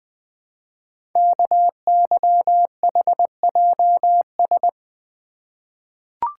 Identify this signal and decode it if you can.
Morse code, heard as a keyed tone, about 20 wpm, 700 Hz. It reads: KYHJS
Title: Code Groups - 5 Letters - 20wpm